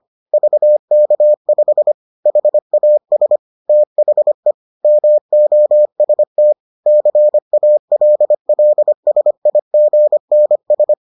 Morse code, heard as a keyed tone, about 25 wpm, 600 Hz. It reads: VK5 HAS THE MOST CALLSIGNS